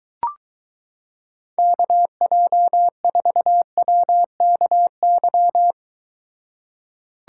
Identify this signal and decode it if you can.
Morse code, heard as a keyed tone, about 22 wpm, 700 Hz. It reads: KJ4WKY